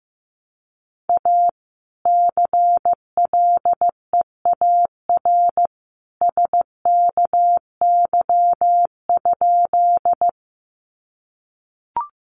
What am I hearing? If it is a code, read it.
Morse code, heard as a keyed tone, about 15 wpm, 700 Hz. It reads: A CLEAR SKY?
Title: Sentences from Top 1200 Words 15wpm